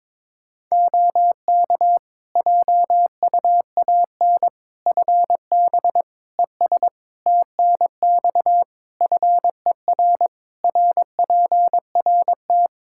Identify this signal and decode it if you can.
Morse code, heard as a keyed tone, about 21 wpm, 700 Hz. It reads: OK JUAN FB ES TNX FER RPRT
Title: K1USN SST QSO Practice 22wpm